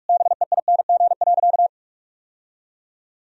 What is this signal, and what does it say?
Morse code, heard as a keyed tone, about 42 wpm, 700 Hz. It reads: BEING.
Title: Sentences from Top 1000 Words - Review Part 3 - 45wpm